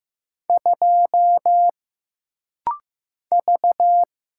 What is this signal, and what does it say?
Morse code, heard as a keyed tone, about 15 wpm, 700 Hz. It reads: 2 V
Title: Single Letter-Number Alt - Rapid-Fire 15wpm